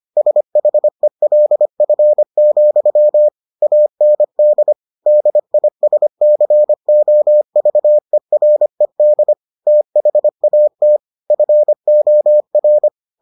Morse code, heard as a keyed tone, about 25 wpm, 600 Hz. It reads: SHELF, AND DISCOVERED THAT FOR